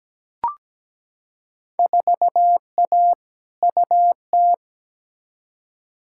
Morse code, heard as a keyed tone, about 17 wpm, 700 Hz. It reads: 4A UT
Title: Field Day Exchanges - 17wpm